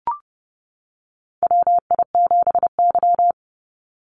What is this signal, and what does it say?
Morse code, heard as a keyed tone, about 32 wpm, 700 Hz. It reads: WI7Y
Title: USA Call Signs 30wpm